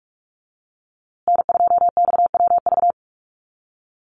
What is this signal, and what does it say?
Morse code, heard as a keyed tone, about 44 wpm, 700 Hz. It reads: N2XWV
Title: USA Call Signs Repeated 3x 45wpm